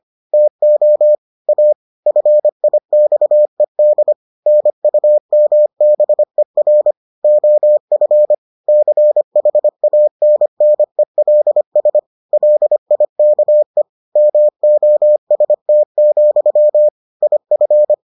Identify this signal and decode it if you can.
Morse code, heard as a keyed tone, about 25 wpm, 600 Hz. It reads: TO A FIXED NUMBER OF CHANNELS LIKE MOST, IF